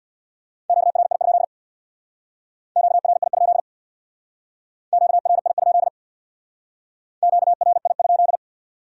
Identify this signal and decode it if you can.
Morse code, heard as a keyed tone, about 64 wpm, 700 Hz. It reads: QRI? QRI? QRI? QRI?